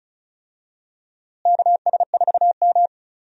Morse code, heard as a keyed tone, about 36 wpm, 700 Hz. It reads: KS4M
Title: USA Call Signs - 4 Characters - Repeated 3x - 35wpm